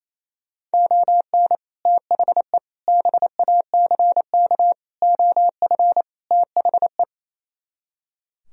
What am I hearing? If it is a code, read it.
Morse code, heard as a keyed tone, about 34 wpm, 700 Hz. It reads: ON THE BACK OF THE
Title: Speed-Racing - Sets of 5 Words 28wpm